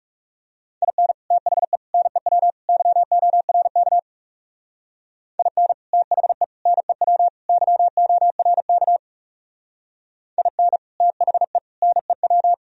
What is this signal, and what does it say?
Morse code, heard as a keyed tone, about 38 wpm, 700 Hz. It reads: IN THE NEW YORK IN THE NEW YORK IN THE NEW
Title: Speed-Racing - Sets of 4 Words 30wpm